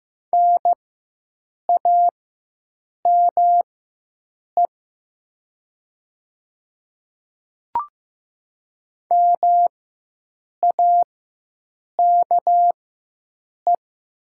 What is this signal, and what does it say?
Morse code, heard as a keyed tone, about 15 wpm, spaced as for 7 wpm, 700 Hz. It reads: NAME MAKE
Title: 4x Character Spacing - Top 300 Words - 15wpm